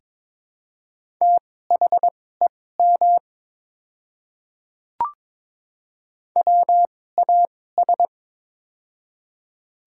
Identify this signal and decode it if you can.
Morse code, heard as a keyed tone, about 22 wpm, spaced as for 16 wpm, 700 Hz. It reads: THEM WAS
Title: 2x Character Spacing - Top 100 Words - 22wpm